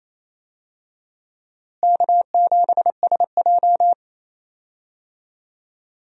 Morse code, heard as a keyed tone, about 28 wpm, 700 Hz. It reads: K7SJ
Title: USA Call Signs - 4 Characters - Repeated 2x - 28wpm